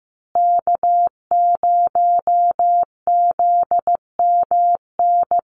Morse code, heard as a keyed tone, about 15 wpm, 700 Hz. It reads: K0ZMN
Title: USA Call Signs Repeated 2x 15wpm